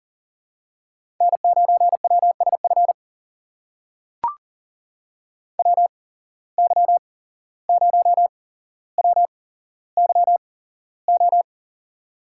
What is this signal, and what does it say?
Morse code, heard as a keyed tone, about 40 wpm, spaced as for 16 wpm, 700 Hz. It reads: N9WSF WY0WYO
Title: USA Call Signs - 8x Character Spacing - 40wpm